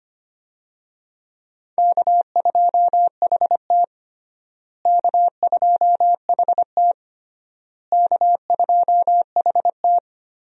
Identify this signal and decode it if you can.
Morse code, heard as a keyed tone, about 25 wpm, 700 Hz. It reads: K2HT K2HT K2HT
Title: USA Call Signs - 4 Characters - Repeated 3x - 25wpm